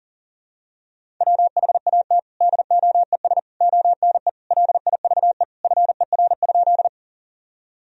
Morse code, heard as a keyed tone, about 37 wpm, 700 Hz. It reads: WHAT DOES ONE LIVE FER?
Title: Sentences from Top 200 Words and Abbreviations 40wpm